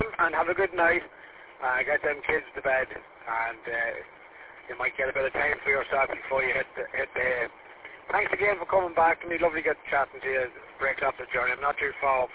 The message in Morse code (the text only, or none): none